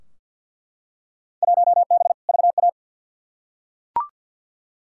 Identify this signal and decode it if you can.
Morse code, heard as a keyed tone, about 47 wpm, 700 Hz. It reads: 1B VA
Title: Field Day Exchanges Encore - 50wpm